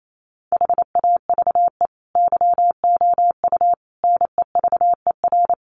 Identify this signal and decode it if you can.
Morse code, heard as a keyed tone, about 28 wpm, 700 Hz. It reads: HAVE YOU NEVER
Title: Sentences from Top 300 Words 28wpm